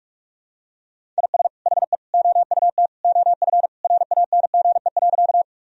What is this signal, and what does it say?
Morse code, heard as a keyed tone, about 46 wpm, 700 Hz. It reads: IS HE OUT OF RANGE.
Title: Sentences from Top 1000 Words - Review Part 4 - 45wpm